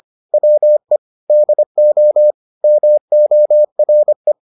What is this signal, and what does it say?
Morse code, heard as a keyed tone, about 25 wpm, 600 Hz. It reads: WE DO MORE